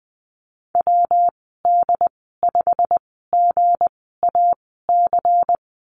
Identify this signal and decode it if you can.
Morse code, heard as a keyed tone, about 20 wpm, spaced as for 14 wpm, 700 Hz. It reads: WD5GAC
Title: 193 Lesson 31g - Letter v - Call Signs 20wpm